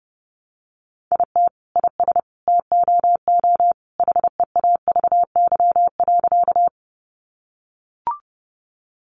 Morse code, heard as a keyed tone, about 30 wpm, 700 Hz. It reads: IT IS TOO HEAVY.
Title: Sentences from Top 1100 Words 30wpm